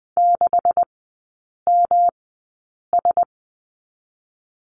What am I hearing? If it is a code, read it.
Morse code, heard as a keyed tone, about 20 wpm, spaced as for 8 wpm, 700 Hz. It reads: 6MS
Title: 127 Lesson 21e - Number 6 - Three Characters 20wpm